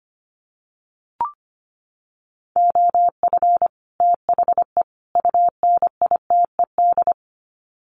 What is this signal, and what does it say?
Morse code, heard as a keyed tone, about 25 wpm, 700 Hz. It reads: OF THE UNITED
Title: Sets of 3 Words 25wpm